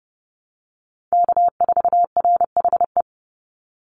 Morse code, heard as a keyed tone, about 30 wpm, 700 Hz. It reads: K4RHE
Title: USA Call Signs 30wpm